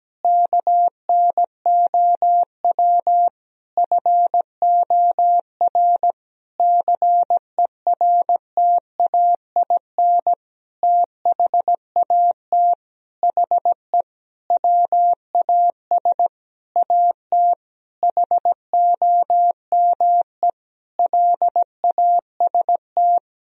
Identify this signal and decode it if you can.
Morse code, heard as a keyed tone, about 17 wpm, 700 Hz. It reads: KNOW FOR CERTAIN THAT HE WAS AT HOME LAST
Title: Sentences from Top 1000 Words - Review Part 2 - 17wpm